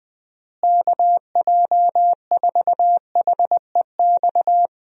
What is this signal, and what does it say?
Morse code, heard as a keyed tone, about 21 wpm, 700 Hz. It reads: KJ4HEX
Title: USA Call Signs - 6 Characters - Repeated 2x - 20wpm